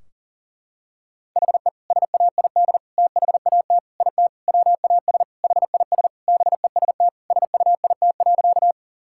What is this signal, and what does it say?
Morse code, heard as a keyed tone, about 40 wpm, 700 Hz. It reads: HE SAID THAT IT WAS HIS BEST SUIT.
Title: Sentences from Top 1600 Words 40wpm